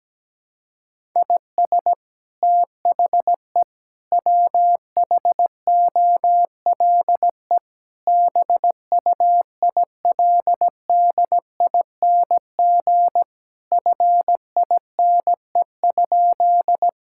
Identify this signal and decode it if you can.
Morse code, heard as a keyed tone, about 17 wpm, 700 Hz. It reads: IS THE WHOLE BUILDING FINE?
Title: Sentences from Top 1000 Words - Review Part 5 - 17wpm